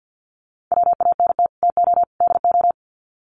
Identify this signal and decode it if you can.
Morse code, heard as a keyed tone, about 50 wpm, 700 Hz. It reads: WANT TO DO
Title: Sets of 3 Words 50wpm